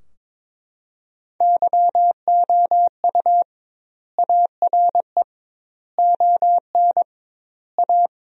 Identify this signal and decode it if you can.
Morse code, heard as a keyed tone, about 22 wpm, 700 Hz. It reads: YOU ARE ON A
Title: 2x Word-Spacing - Sentences from Top 400 Words - 22wpm